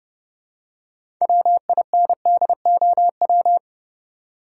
Morse code, heard as a keyed tone, about 30 wpm, 700 Hz. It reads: WINDOW
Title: Six Letter Words - 30wpm